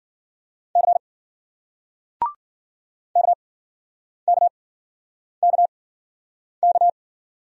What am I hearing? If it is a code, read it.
Morse code, heard as a keyed tone, about 56 wpm, spaced as for 9 wpm, 700 Hz. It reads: C KKKK